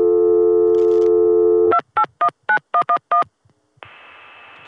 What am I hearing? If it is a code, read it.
DTMF, heard as telephone touch tones, tones of about 90 ms. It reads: 2819111